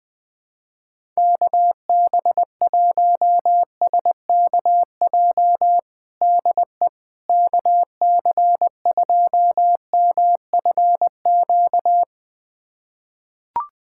Morse code, heard as a keyed tone, about 20 wpm, 700 Hz. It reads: KB1SKJ DE KC2MFQ
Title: USA Callsign DE Callsign 20wpm